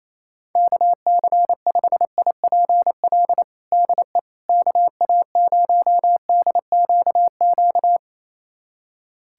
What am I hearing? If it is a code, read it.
Morse code, heard as a keyed tone, about 28 wpm, 700 Hz. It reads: KC5IPL DE KA0DQQ
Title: USA Callsign DE Callsign - 6 Characters - 28wpm